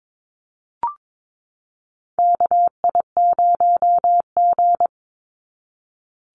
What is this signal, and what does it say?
Morse code, heard as a keyed tone, about 22 wpm, 700 Hz. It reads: KI0G
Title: NA SOTA Chaser Callsigns 22wpm